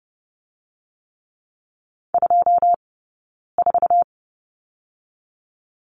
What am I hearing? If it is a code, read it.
Morse code, heard as a keyed tone, about 30 wpm, 700 Hz. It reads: 2 4
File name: Lesson 17d - Letter u - Two Characters 30wpm